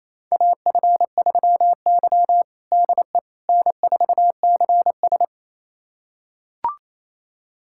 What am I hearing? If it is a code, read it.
Morse code, heard as a keyed tone, about 28 wpm, 700 Hz. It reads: AF3Y DE N4CS
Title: USA Callsign DE Callsign - 4 Characters - 28wpm